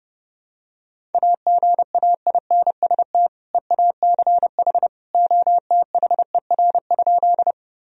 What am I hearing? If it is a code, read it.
Morse code, heard as a keyed tone, about 30 wpm, 700 Hz. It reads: AGAINST EACH OTHER?
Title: Sentences from Top 1000 Words - Review Part 1 - 30wpm